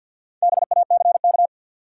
Morse code, heard as a keyed tone, about 53 wpm, 700 Hz. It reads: BACK
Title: Sets of 2 Words - New Word Drill - 50wpm